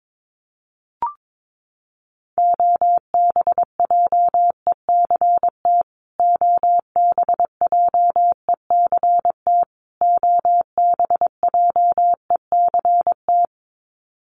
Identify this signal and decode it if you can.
Morse code, heard as a keyed tone, about 22 wpm, 700 Hz. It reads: OBJECT OBJECT OBJECT
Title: Top 500 Words Repeated 22wpm